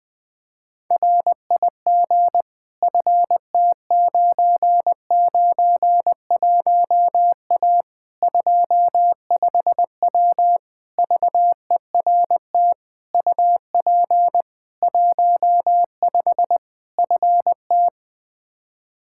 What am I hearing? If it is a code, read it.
Morse code, heard as a keyed tone, about 20 wpm, 700 Hz. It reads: RIG FT991A 25W VERT UP 15 FT
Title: Practice QSO - Exchange 2 - Info - 20wpm